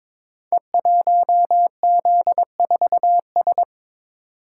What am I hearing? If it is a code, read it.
Morse code, heard as a keyed tone, about 22 wpm, 700 Hz. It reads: E1Z4S
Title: Code Groups - 5 Letter:Numbers - 22wpm